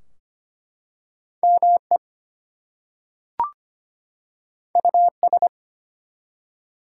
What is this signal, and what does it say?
Morse code, heard as a keyed tone, about 25 wpm, 700 Hz. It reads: ME US
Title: Two Letter Words - 25wpm